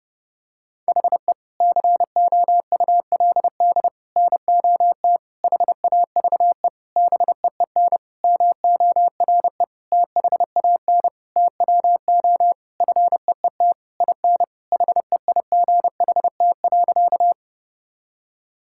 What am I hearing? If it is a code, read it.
Morse code, heard as a keyed tone, about 32 wpm, 700 Hz. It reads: HE COULD NOT HAVE BEEN MORE THAN TWO FEET IN HEIGHT.
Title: Sentences from Top 1600 Words 30wpm